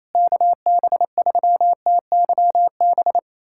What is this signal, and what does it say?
Morse code, heard as a keyed tone, about 28 wpm, 700 Hz. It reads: KB3TYB
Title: USA Callsign DE Callsign 28wpm